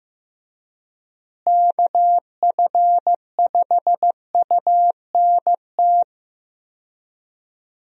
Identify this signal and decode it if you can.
Morse code, heard as a keyed tone, about 15 wpm, 700 Hz. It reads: KF5UNT
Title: USA Call Signs - 6 Characters - Repeated 3x - 15wpm